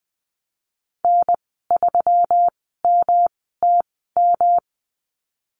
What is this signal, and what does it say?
Morse code, heard as a keyed tone, about 20 wpm, spaced as for 15 wpm, 700 Hz. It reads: N3MTM
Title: 128 Lesson 21g - Number 6 - Call Signs 20wpm